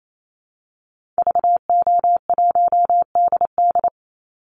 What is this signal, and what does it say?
Morse code, heard as a keyed tone, about 28 wpm, 700 Hz. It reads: VO1DD
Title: Call Signs - 28wpm